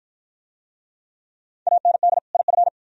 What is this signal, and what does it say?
Morse code, heard as a keyed tone, about 48 wpm, 700 Hz. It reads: AND IF